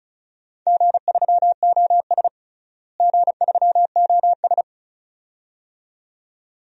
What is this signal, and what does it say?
Morse code, heard as a keyed tone, about 34 wpm, 700 Hz. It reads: G3OS G3OS
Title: DXCC-100 Call Signs - 4 Characters - Repeated 2x - 35wpm